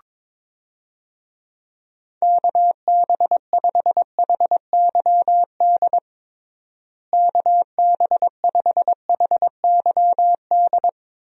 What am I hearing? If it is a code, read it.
Morse code, heard as a keyed tone, about 22 wpm, 700 Hz. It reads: KB5HYD KB5HYD
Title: USA Call Signs - 6 Characters - Repeated 2x - 22wpm